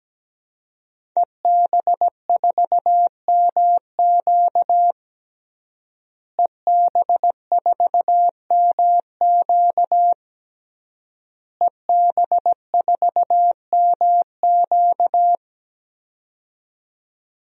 Morse code, heard as a keyed tone, about 17 wpm, 700 Hz. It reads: EB4MQ EB4MQ EB4MQ
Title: DXCC-100 Call Signs - 5 Characters - Repeated 3x - 17wpm